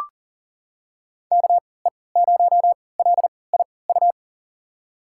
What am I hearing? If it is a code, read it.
Morse code, heard as a keyed tone, about 40 wpm, spaced as for 23 wpm, 700 Hz. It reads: KE0LIU
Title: USA Call Signs - 6 Characters - 3x Character Spacing - 40wpm